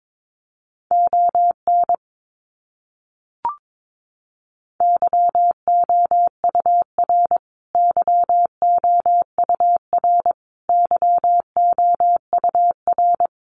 Morse code, heard as a keyed tone, about 22 wpm, 700 Hz. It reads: ON YOUR YOUR YOUR
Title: Top 200 Words Repeated 22wpm